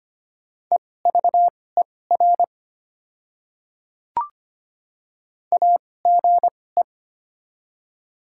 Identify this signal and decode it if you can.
Morse code, heard as a keyed tone, about 25 wpm, spaced as for 18 wpm, 690 Hz. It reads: EVER AGE
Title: 2x Character Spacing - Top 500 Words - 25wpm